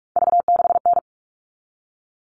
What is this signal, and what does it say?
Morse code, heard as a keyed tone, about 44 wpm, 700 Hz. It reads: V6N
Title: All Cut Numbers - 3-Digit 45wpm